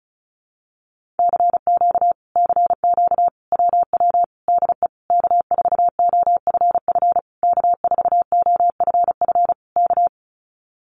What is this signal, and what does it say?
Morse code, heard as a keyed tone, about 34 wpm, 700 Hz. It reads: CQ CQ WW DE K4OFF K4OFF K
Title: Common QSO Phrases Encore 35wpm